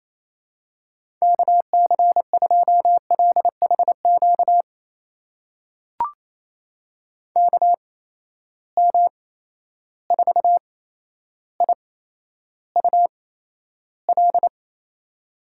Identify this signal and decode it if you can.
Morse code, heard as a keyed tone, about 27 wpm, spaced as for 12 wpm, 700 Hz. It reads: KC2LHQ KM4IUL
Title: USA Call Signs - 6 Characters - 8x Character Spacing - 28wpm